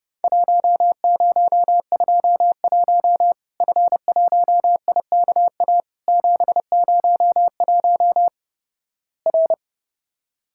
Morse code, heard as a keyed tone, about 30 wpm, 700 Hz. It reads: 1021 F1IKA 701 R
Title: Worked All Europe DX Contest - QTC Exchange - 30wpm